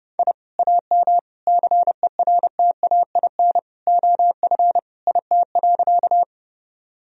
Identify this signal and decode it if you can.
Morse code, heard as a keyed tone, about 31 wpm, 700 Hz. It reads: I AM CERTAIN OF IT.
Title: Sentences from Top 1000 Words - Review Part 2 - 30wpm